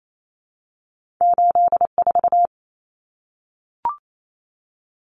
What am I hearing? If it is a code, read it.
Morse code, heard as a keyed tone, about 28 wpm, 700 Hz. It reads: 8 4
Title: Common Cut Numbers - 2-Digit 28wpm